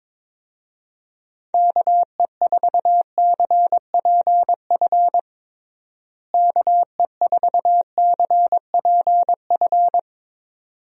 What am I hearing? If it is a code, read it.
Morse code, heard as a keyed tone, about 22 wpm, 700 Hz. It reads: KE4CPF KE4CPF
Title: USA Call Signs - 6 Characters - Repeated 2x - 22wpm